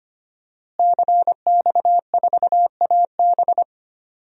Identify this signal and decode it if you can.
Morse code, heard as a keyed tone, about 25 wpm, 700 Hz. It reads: CX4AB
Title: DXCC-100 Call Signs - 5 Characters - Repeated 3x - 25wpm